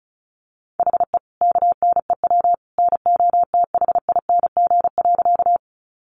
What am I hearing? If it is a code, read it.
Morse code, heard as a keyed tone, about 34 wpm, 700 Hz. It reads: HE KNEW NOTHING.
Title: Sentences from Top 500 Words 35wpm